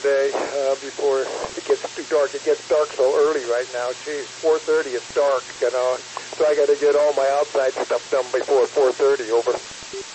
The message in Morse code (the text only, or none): none